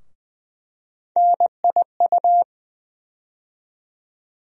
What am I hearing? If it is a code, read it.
Morse code, heard as a keyed tone, about 20 wpm, 700 Hz. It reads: NIU
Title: USA Call Sign Postfixes - 3 Letters 20wpm